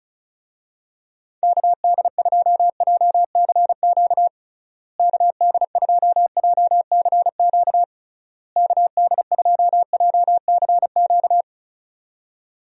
Morse code, heard as a keyed tone, about 36 wpm, 700 Hz. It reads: KD2JCQ KD2JCQ KD2JCQ
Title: USA Call Signs - 6 Characters - Repeated 3x - 35wpm